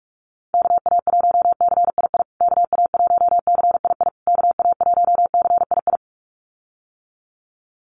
Morse code, heard as a keyed tone, about 44 wpm, 700 Hz. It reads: KA1CII KA1CII KA1CII
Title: USA Call Signs Repeated 3x 45wpm